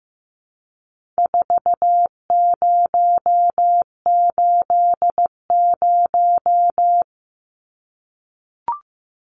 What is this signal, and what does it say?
Morse code, heard as a keyed tone, about 15 wpm, 700 Hz. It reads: 4080
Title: Numbers - 4-Digit 15wpm